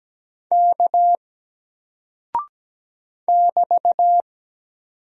Morse code, heard as a keyed tone, about 17 wpm, 700 Hz. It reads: K <BT>
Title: Single Character (Letter:Number:Punctuation:Prosign) 17wpm